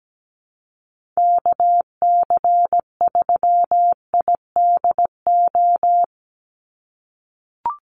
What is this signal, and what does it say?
Morse code, heard as a keyed tone, about 17 wpm, 700 Hz. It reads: KC3IDO
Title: USA Call Signs 17wpm